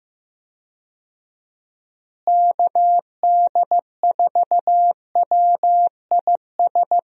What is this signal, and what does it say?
Morse code, heard as a keyed tone, about 15 wpm, 700 Hz. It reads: KD4WIS
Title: USA Call Signs - 6 Characters - Repeated 2x - 15wpm